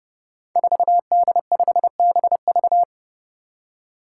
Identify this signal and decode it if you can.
Morse code, heard as a keyed tone, about 28 wpm, 700 Hz. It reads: 4D5BV
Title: Five Challenging Letter-Number Combinations 30wpm